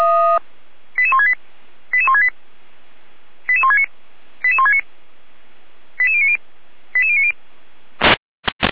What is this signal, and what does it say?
Five-tone selective call (ZVEI), heard as quick runs of five tones, tones of about 70 ms. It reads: 88168 88168 88169 88169 88090 88090